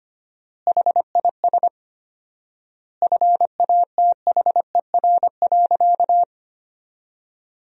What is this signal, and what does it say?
Morse code, heard as a keyed tone, about 25 wpm, 700 Hz. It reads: HIS FATHER.
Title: 4x Word-Spacing - Sentences from Top 500 Words - 25wpm